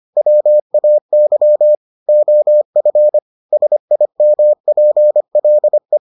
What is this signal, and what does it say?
Morse code, heard as a keyed tone, about 25 wpm, 600 Hz. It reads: WAY OF SIMPLE